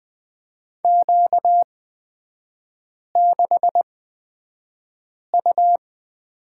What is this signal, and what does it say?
Morse code, heard as a keyed tone, about 20 wpm, spaced as for 5 wpm, 700 Hz. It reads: Q6U
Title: 236 Lesson 38e - Letter q - Three Characters 20wpm